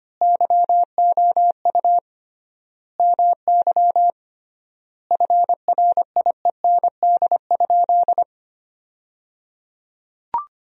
Morse code, heard as a keyed tone, about 25 wpm, 700 Hz. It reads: YOU MY FRIEND?
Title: 3x Word-Spacing - Sentences from Top 300 Words - 25wpm